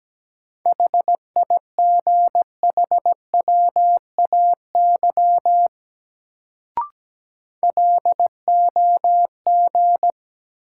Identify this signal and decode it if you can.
Morse code, heard as a keyed tone, about 17 wpm, 700 Hz. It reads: HIGHWAY LOG
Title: Sentences from Top 1400 Words - New Word Drill Alt - 17wpm